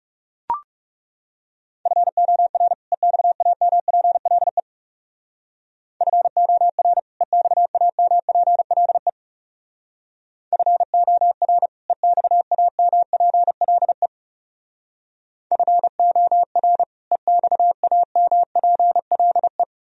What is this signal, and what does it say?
Morse code, heard as a keyed tone, about 40 wpm, 700 Hz. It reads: FOR EXAMPLE FOR EXAMPLE FOR EXAMPLE FOR EXAMPLE